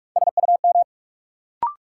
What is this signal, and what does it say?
Morse code, heard as a keyed tone, about 46 wpm, 700 Hz. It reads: SUM
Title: Sentences from Top 1000 Words - Review Part 6 - New Word Drill - 45wpm